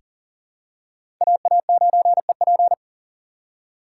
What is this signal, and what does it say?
Morse code, heard as a keyed tone, about 40 wpm, 700 Hz. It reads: AA9EP
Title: USA Call Signs - 5 Characters - 2x Character Spacing - 40wpm